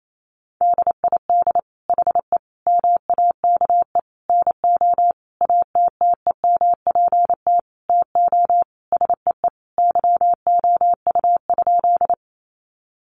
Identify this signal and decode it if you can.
Morse code, heard as a keyed tone, about 28 wpm, 700 Hz. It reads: DID HE MAKE NO ATTEMPT TO SEE YOU?
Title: Sentences from Top 1000 Words 28wpm